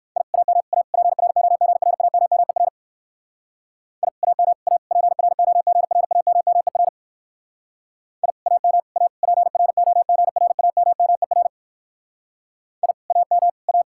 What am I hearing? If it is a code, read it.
Morse code, heard as a keyed tone, about 57 wpm, 700 Hz. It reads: I AM A PROGRAMMER I AM A PROGRAMMER I AM A PROGRAMMER I AM A